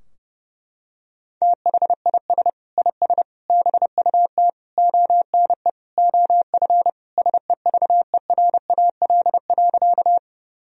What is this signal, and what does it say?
Morse code, heard as a keyed tone, about 32 wpm, 700 Hz. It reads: THIS IS BUT ONE OF SEVERAL.
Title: Sentences from Top 1000 Words - Review Part 2 - 30wpm